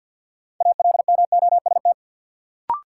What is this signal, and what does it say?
Morse code, heard as a keyed tone, about 50 wpm, 700 Hz. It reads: ALMOST
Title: Sentences from Top 700 Words - New Word Drill - 50wpm